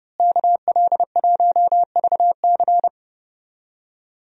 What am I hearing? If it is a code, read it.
Morse code, heard as a keyed tone, about 30 wpm, 700 Hz. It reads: KL1VC
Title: USA Call Signs - 5 Characters - Repeated 2x - 30wpm